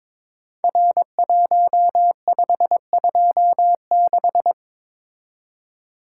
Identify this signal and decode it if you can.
Morse code, heard as a keyed tone, about 22 wpm, 700 Hz. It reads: R1526
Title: Code Groups - 5 Letter:Numbers - 22wpm